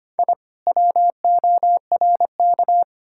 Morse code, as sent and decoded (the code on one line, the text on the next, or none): .. / .-- --- .-. -.-
I WORK